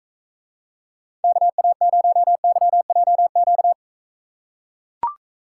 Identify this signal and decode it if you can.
Morse code, heard as a keyed tone, about 41 wpm, 700 Hz. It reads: KA0YJQ